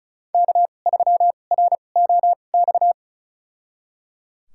Morse code, heard as a keyed tone, about 36 wpm, spaced as for 25 wpm, 700 Hz. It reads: K3ROX
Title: USA Call Signs - 5 Characters - 2x Character Spacing - 35wpm